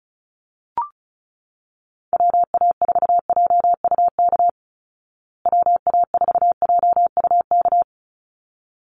Morse code, heard as a keyed tone, about 34 wpm, 700 Hz. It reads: WA4JUK WA4JUK
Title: Call Signs Repeated 3x - 35wpm